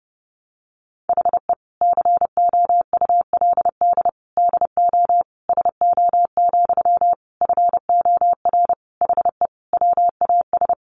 Morse code, heard as a keyed tone, about 30 wpm, 700 Hz. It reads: HE COULD DO SO, FOR HE WAS
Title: Sentences from Top 1500 Words 30wpm